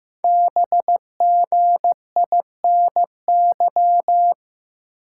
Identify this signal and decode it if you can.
Morse code, heard as a keyed tone, about 15 wpm, 700 Hz. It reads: BGINY
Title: Code Groups - 5 Letters - 15wpm